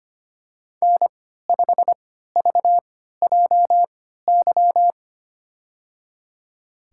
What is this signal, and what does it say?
Morse code, heard as a keyed tone, about 25 wpm, spaced as for 14 wpm, 700 Hz. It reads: N5VJY
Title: USA Call Signs - 5 Characters - 3x Character Spacing - 25wpm